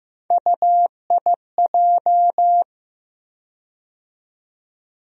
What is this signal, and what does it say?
Morse code, heard as a keyed tone, about 15 wpm, 700 Hz. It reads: UIJ
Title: USA Call Sign Postfixes - 3 Letters 15wpm